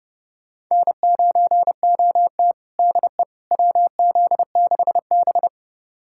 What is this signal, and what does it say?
Morse code, heard as a keyed tone, about 32 wpm, 700 Hz. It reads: N9OT DE WZ6B
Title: USA Callsign DE Callsign - 4 Characters - 30wpm